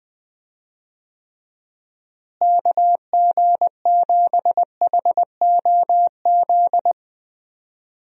Morse code, heard as a keyed tone, about 21 wpm, 700 Hz. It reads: KG7HOZ
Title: USA Call Signs - 6 Characters - Repeated 3x - 20wpm